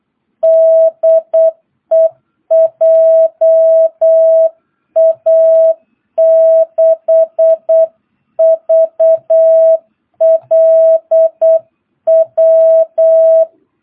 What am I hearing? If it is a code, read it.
Morse code, heard as a keyed tone, about 8 wpm, 660 Hz. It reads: DE JA6VLW
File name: jh8cbh / 2022cw